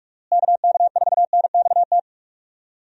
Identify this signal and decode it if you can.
Morse code, heard as a keyed tone, about 41 wpm, 700 Hz. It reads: KK4NXT
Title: USA Call Signs - 6 Characters - Repeated 2x - 45wpm